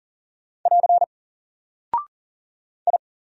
Morse code, heard as a keyed tone, about 40 wpm, 700 Hz. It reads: <AR> I
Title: Single Character (Letter:Number:Punctuation:Prosign) Alt 40wpm